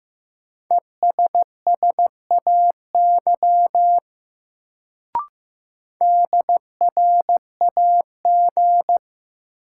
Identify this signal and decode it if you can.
Morse code, heard as a keyed tone, about 15 wpm, 700 Hz. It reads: ESSAY DRAG
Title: Sentences from Top 1500 Words - New Word Drill Alt - 15wpm